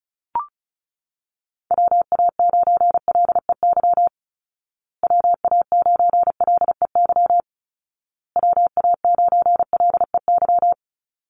Morse code, heard as a keyed tone, about 34 wpm, 700 Hz. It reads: WA9LEY WA9LEY WA9LEY
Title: Call Signs Repeated 2x - 35wpm